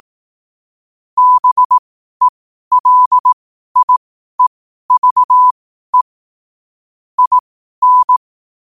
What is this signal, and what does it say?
Morse code, heard as a keyed tone, about 18 wpm, spaced as for 12 wpm, 1000 Hz. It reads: BELIEVE IN
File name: qotdfiles / 12wpm